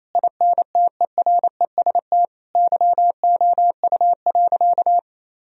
Morse code, heard as a keyed tone, about 28 wpm, 700 Hz. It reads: INTEREST YOU.
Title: Sentences from Top 1000 Words - Review Part 2 - 28wpm